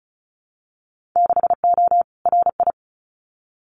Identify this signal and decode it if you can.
Morse code, heard as a keyed tone, about 36 wpm, 700 Hz. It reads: 6O RI
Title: Winter Field Day Exchanges Encore 35wpm